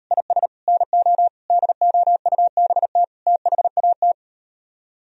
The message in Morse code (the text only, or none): IS NO DOUBT THAT